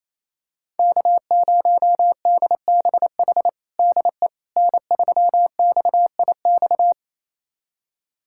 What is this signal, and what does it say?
Morse code, heard as a keyed tone, about 28 wpm, 700 Hz. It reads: K0DBH DE N3XIX
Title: USA Callsign DE Callsign - 5 Characters - 28wpm